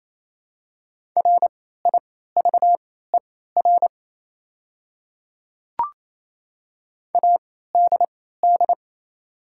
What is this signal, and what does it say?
Morse code, heard as a keyed tone, about 28 wpm, spaced as for 16 wpm, 700 Hz. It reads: RIVER ADD
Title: 3x Character Spacing - Top 300 Words - 28wpm